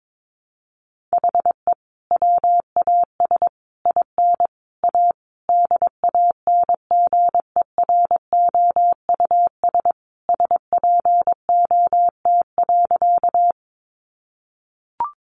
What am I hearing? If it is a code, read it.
Morse code, heard as a keyed tone, about 22 wpm, 700 Hz. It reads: HE WAS IN A DANGEROUS SPOT.
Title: Sentences from Top 1200 Words 22wpm